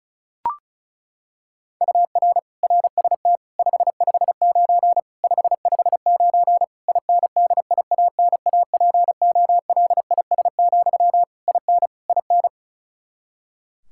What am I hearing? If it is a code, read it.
Morse code, heard as a keyed tone, about 34 wpm, 700 Hz. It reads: UR RST 559 559 INDIANAPOLIS, IN IN
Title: Practice QSO - Exchange 1 - Beginning - 35wpm